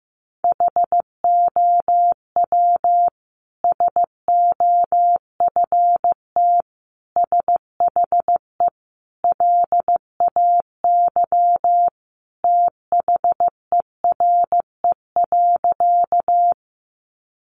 Morse code, heard as a keyed tone, about 15 wpm, 700 Hz. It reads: HOW SOFT SHE LAY THERE.
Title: Sentences from Top 1500 Words 15wpm